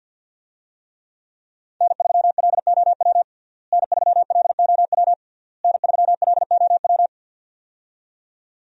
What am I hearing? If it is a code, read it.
Morse code, heard as a keyed tone, about 52 wpm, 700 Hz. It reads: N3LOW N3LOW N3LOW
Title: USA Call Signs - 5 Characters - Repeated 3x - 50wpm